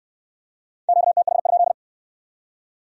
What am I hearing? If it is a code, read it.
Morse code, heard as a keyed tone, about 58 wpm, 700 Hz. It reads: QTH?